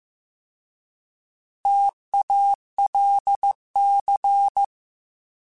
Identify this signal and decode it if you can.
Morse code, heard as a keyed tone, about 15 wpm, 800 Hz. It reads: TALC
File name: T Prefixes / T mp3s